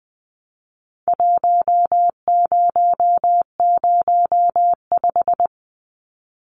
Morse code, heard as a keyed tone, about 20 wpm, 700 Hz. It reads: 1005
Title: Numbers - Contest Style - 4-Digit 20wpm